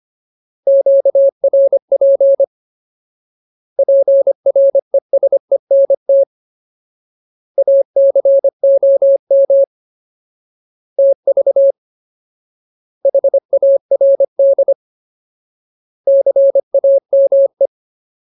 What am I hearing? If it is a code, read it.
Morse code, heard as a keyed tone, about 25 wpm, 550 Hz. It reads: QRP PRESENT ACOM TV HARD CAME
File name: Ultimate Word List / 25WPM Ultimate